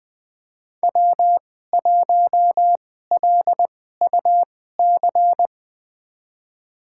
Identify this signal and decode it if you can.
Morse code, heard as a keyed tone, about 20 wpm, spaced as for 15 wpm, 700 Hz. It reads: W1LUC
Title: USA Call Signs - 2x Character Spacing - 20wpm